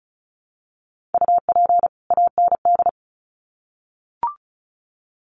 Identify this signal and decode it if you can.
Morse code, heard as a keyed tone, about 34 wpm, 700 Hz. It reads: UP AND
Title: Sets of 2 Words 35wpm